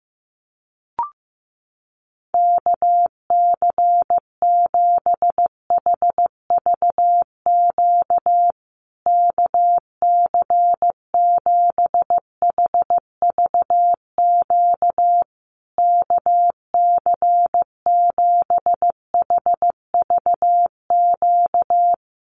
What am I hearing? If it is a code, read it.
Morse code, heard as a keyed tone, about 15 wpm, 700 Hz. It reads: KC7HVQ KC7HVQ KC7HVQ
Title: USA Call Signs Repeated 3x 15wpm